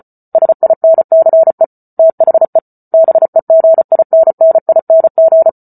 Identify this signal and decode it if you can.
Morse code, heard as a keyed tone, about 34 wpm, 650 Hz. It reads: SINCE THE BEGINNING
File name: Book Two / 08-Chapter 8 - 34 WPM